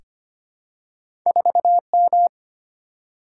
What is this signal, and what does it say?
Morse code, heard as a keyed tone, about 25 wpm, 700 Hz. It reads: 4M
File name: Two Letter-Number Combinations - 25wpm